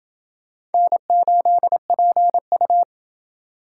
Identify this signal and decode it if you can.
Morse code, heard as a keyed tone, about 28 wpm, 700 Hz. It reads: N8PU